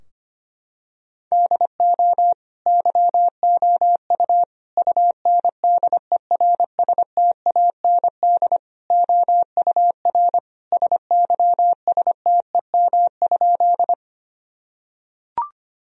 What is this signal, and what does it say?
Morse code, heard as a keyed tone, about 25 wpm, 700 Hz. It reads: DO YOU UNDERSTAND OUR SYSTEM?
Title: Sentences from Top 1000 Words - Review Part 3 - 25wpm